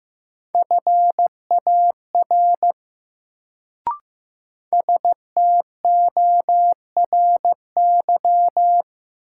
Morse code, heard as a keyed tone, about 15 wpm, 700 Hz. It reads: FAR STORY
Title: Sets of 3 Words - New Word Drill - 15wpm